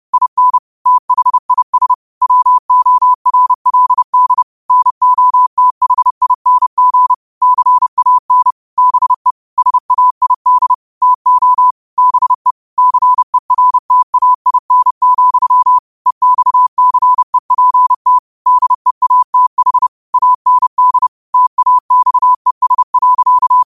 Morse code, heard as a keyed tone, about 30 wpm, 1000 Hz. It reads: IN THIS WORLD NOTHING CAN BE SAID TO BE CERTAIN, EXCEPT DEATH AND TAXES.
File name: qotdfiles / 30wpm